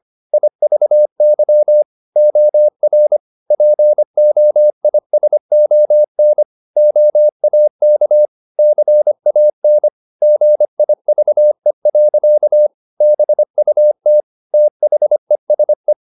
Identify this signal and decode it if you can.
Morse code, heard as a keyed tone, about 25 wpm, 600 Hz. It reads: IVY OR POISON OAK CAN GIVE. BUT THESE
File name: the old man and the sea part 1 25wpm